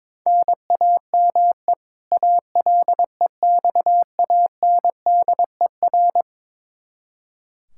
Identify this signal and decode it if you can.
Morse code, heard as a keyed tone, about 22 wpm, 700 Hz. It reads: NAME ALEXANDER
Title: Practice QSO - Exchange 1 - Beginning - 22wpm